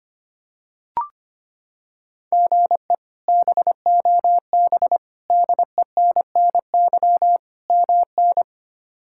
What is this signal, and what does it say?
Morse code, heard as a keyed tone, about 25 wpm, 700 Hz. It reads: GE BOB DENNY MN
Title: K1USN SST QSO Practice - 25wpm